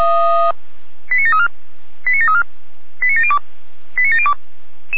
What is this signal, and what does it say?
Five-tone selective call (ZVEI), heard as quick runs of five tones, tones of about 70 ms. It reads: 89734 89734 89702 89702